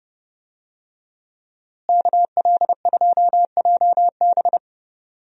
Morse code, heard as a keyed tone, about 31 wpm, 700 Hz. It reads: KL2JB
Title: USA Call Signs - 5 Characters - Repeated 3x - 30wpm